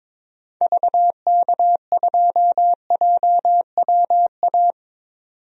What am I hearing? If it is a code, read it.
Morse code, heard as a keyed tone, about 22 wpm, 700 Hz. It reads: VK2JWA
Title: DXCC-100 Call Signs - 6 Characters - 22wpm